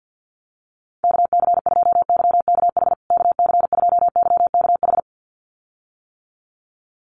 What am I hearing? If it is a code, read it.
Morse code, heard as a keyed tone, about 50 wpm, 700 Hz. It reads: KC2YKH KC2YKH
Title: USA Call Signs Repeated 2x 50wpm